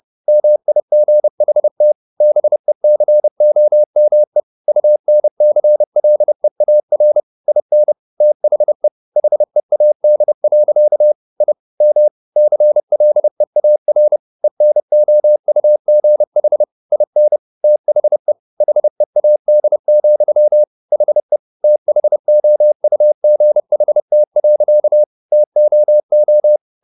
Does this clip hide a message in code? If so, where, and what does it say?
Yes: Morse code from 0.3 to 26.6 s, heard as a keyed tone, about 30 wpm, 600 Hz. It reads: MIGHT BECOME UNCLEAR IN THE HEAD. I M CLEAR ENOUGH IN THE HEAD, HE THOUGHT. TOO